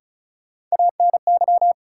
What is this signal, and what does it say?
Morse code, heard as a keyed tone, about 36 wpm, 700 Hz. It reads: ANY
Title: Sets of 3 Words - New Word Drill - 35wpm